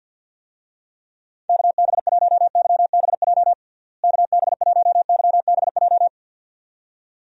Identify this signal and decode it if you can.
Morse code, heard as a keyed tone, about 51 wpm, 700 Hz. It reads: KB1YBJ KB1YBJ
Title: USA Call Signs - 6 Characters - Repeated 2x - 50wpm